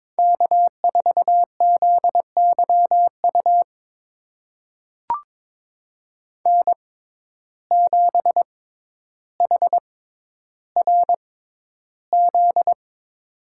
Morse code, heard as a keyed tone, about 22 wpm, spaced as for 11 wpm, 700 Hz. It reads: K4ZYU N7HRZ
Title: USA Call Signs - 5 Characters - 6x Character Spacing - 22wpm